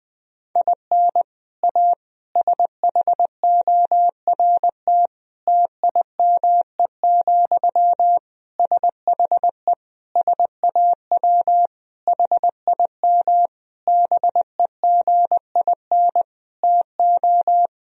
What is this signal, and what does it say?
Morse code, heard as a keyed tone, about 21 wpm, 700 Hz. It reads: IN A SHORT TIME, SHE SAW HIM BEGIN TO
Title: Sentences from Top 1000 Words - Review Part 2 - 20wpm